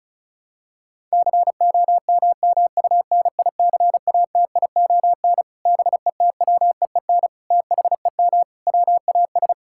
Morse code, heard as a keyed tone, about 36 wpm, 700 Hz. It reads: COMMUNICATION BETWEEN THEM WAS
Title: Sentences from Top 1000 Words - Review Part 6 - 35wpm